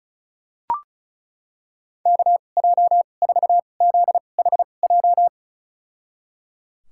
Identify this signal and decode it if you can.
Morse code, heard as a keyed tone, about 34 wpm, spaced as for 25 wpm, 700 Hz. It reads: KJ4ZHJ
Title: USA Call Signs - 6 Characters - 2x Character Spacing - 35wpm